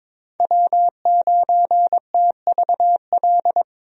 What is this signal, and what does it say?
Morse code, heard as a keyed tone, about 22 wpm, 700 Hz. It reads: W9TVL
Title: USA Call Signs - 5 Characters - Repeated 2x - 22wpm